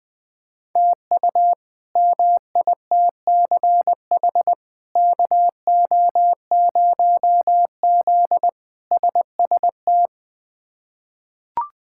Morse code, heard as a keyed tone, about 21 wpm, 700 Hz. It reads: TU MITCH KO0Z SST
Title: K1USN SST QSO Practice - 20wpm